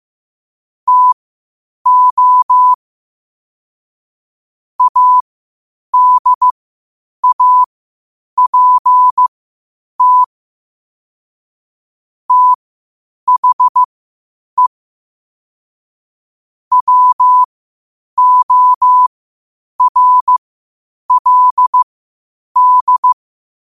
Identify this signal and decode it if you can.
Morse code, heard as a keyed tone, about 15 wpm, spaced as for 8 wpm, 1000 Hz. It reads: TO ADAPT THE WORLD
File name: qotdfiles / 8wpm